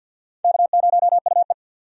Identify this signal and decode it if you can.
Morse code, heard as a keyed tone, about 52 wpm, 700 Hz. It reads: K0UE
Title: USA Call Signs - 4 Characters - 50wpm